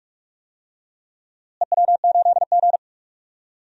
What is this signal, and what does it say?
Morse code, heard as a keyed tone, about 44 wpm, 700 Hz. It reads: EW8G
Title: DXCC-100 Call Signs - 4 Characters - Repeated 3x - 45wpm